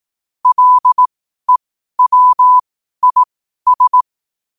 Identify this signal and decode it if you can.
Morse code, heard as a keyed tone, about 18 wpm, spaced as for 12 wpm, 1000 Hz. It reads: LEWIS
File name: qotdfiles / 12wpm